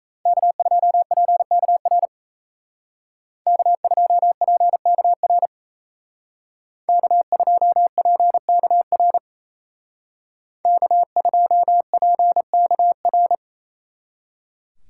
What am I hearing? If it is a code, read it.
Morse code, heard as a keyed tone, about 34 wpm, 700 Hz. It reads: K2PKR K2PKR K2PKR K2PKR
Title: Speed Racing - USA 1x3 Call Signs 28wpm